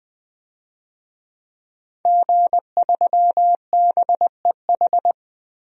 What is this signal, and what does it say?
Morse code, heard as a keyed tone, about 20 wpm, 700 Hz. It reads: G3BEH
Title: DXCC-100 Call Signs - 5 Characters - 20wpm